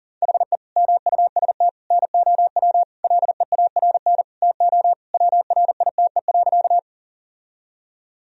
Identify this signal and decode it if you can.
Morse code, heard as a keyed tone, about 40 wpm, 700 Hz. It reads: HE MUST NOW LEARN TO WRITE.
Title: Sentences from Top 1000 Words - Review Part 1 - 40wpm